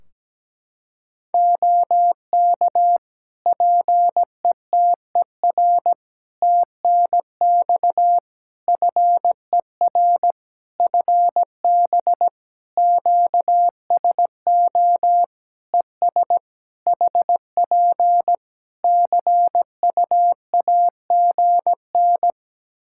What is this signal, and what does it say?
Morse code, heard as a keyed tone, about 17 wpm, 700 Hz. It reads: OK PETER TNX FER FB QSO ES HP CUAGN
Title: K1USN SST QSO Practice 17wpm